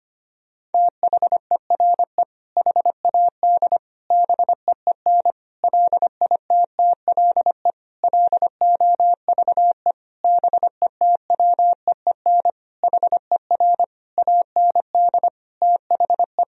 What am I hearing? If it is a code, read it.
Morse code, heard as a keyed tone, about 25 wpm, 700 Hz. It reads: THERE HAD BEEN LITTLE LOVE BETWEEN HER AND THE
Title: Sentences from Top 1000 Words - Review Part 1 - 25wpm